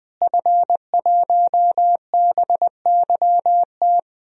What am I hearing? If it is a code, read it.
Morse code, heard as a keyed tone, about 20 wpm, 700 Hz. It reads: F1BYT
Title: DXCC-100 Call Signs - 5 Characters - Repeated 3x - 20wpm